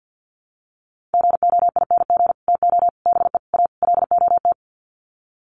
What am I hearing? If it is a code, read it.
Morse code, heard as a keyed tone, about 50 wpm, 700 Hz. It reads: GOING TO BE A LOT
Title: Sets of 5 Words Encore 50wpm